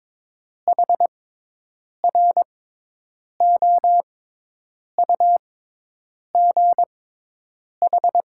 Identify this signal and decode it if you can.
Morse code, heard as a keyed tone, about 22 wpm, spaced as for 8 wpm, 700 Hz. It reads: HROUGH
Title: 6x Character Spacing - Top 200 Words - 22wpm